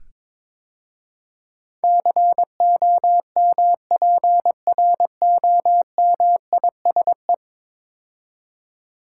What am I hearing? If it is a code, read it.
Morse code, heard as a keyed tone, about 22 wpm, 700 Hz. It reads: COMPROMISE
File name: practice / 22wpm / 10 Letter Words - First 3 Letters Given - 22wpm